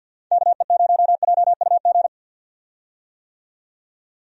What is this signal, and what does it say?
Morse code, heard as a keyed tone, about 52 wpm, 700 Hz. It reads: KE0JUG
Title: USA Call Signs - 6 Characters - 50wpm